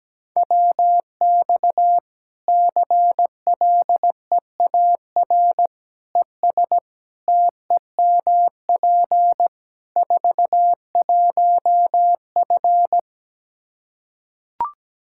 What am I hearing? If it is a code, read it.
Morse code, heard as a keyed tone, about 17 wpm, 700 Hz. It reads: WX CLEAR ES TEMP 41F
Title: K1USN SST QSO Practice 17wpm